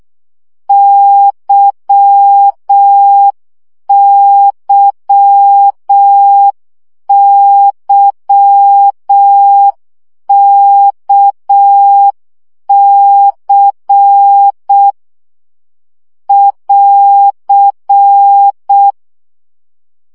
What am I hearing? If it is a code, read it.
Morse code, heard as a keyed tone, about 6 wpm, 800 Hz. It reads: YYYKC <AR>
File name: CW gyakorlas k c y 5 baud